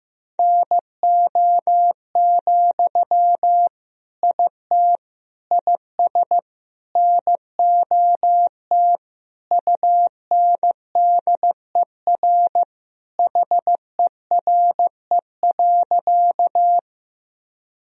Morse code, heard as a keyed tone, about 15 wpm, 700 Hz. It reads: NO, IT IS NOT UNDER HERE.
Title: Sentences from Top 1000 Words - Review Part 1 - 15wpm